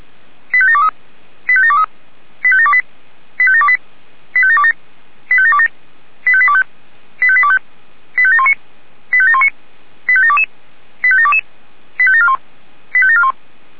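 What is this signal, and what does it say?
Five-tone selective call (ZVEI), heard as quick runs of five tones, tones of about 70 ms. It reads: 85723 85723 85728 85728 85727 85727 85825 85825 85719 85719 85722 85722 85741 85741
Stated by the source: ZVEI five-tone selective call 85721 received by SDR.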